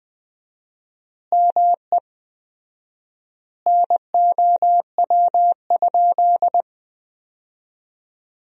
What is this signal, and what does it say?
Morse code, heard as a keyed tone, about 20 wpm, 700 Hz. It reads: ME NOW?
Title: 4x Word-Spacing - Sentences from Top 400 Words - 20wpm